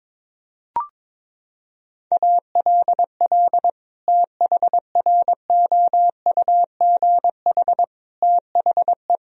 Morse code, heard as a keyed tone, about 22 wpm, 700 Hz. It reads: ALL THROUGH THE
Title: Sentences from Top 1000 Words - Review Part 1 - 22wpm